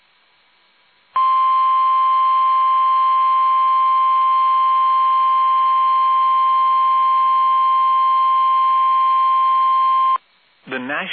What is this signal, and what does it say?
Emergency Alert System (SAME) tones (the 1050 Hz attention tone), with speech. It is heard from 1.1 to 10.2 s.